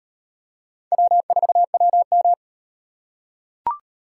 Morse code, heard as a keyed tone, about 34 wpm, 700 Hz. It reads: W4WM